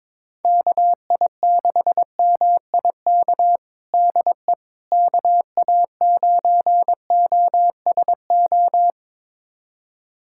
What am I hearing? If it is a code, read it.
Morse code, heard as a keyed tone, about 22 wpm, 700 Hz. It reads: KI6MIK DE KA9OSO
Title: USA Callsign DE Callsign - 6 Characters - 22wpm